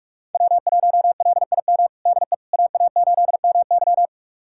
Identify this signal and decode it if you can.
Morse code, heard as a keyed tone, about 46 wpm, 700 Hz. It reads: W1LIM DE AA8MY
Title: USA Callsign DE Callsign - 5 Characters - 45wpm